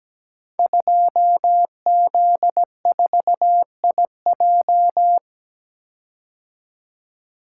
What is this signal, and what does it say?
Morse code, heard as a keyed tone, about 17 wpm, 700 Hz. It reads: 2Z4IJ
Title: Code Groups - 5 Letter:Numbers - 17wpm